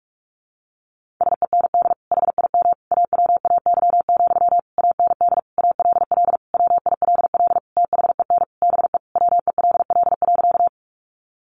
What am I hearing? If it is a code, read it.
Morse code, heard as a keyed tone, about 44 wpm, 700 Hz. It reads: SEND HIM AWAY, AND ALL WILL THEN BE WELL.
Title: Sentences from Top 1500 Words 45wpm